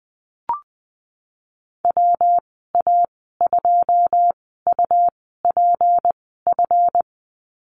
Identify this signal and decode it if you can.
Morse code, heard as a keyed tone, about 20 wpm, spaced as for 14 wpm, 700 Hz. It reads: WA2UPF
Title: 186 Lesson 30g - Letter b - Call Signs 20wpm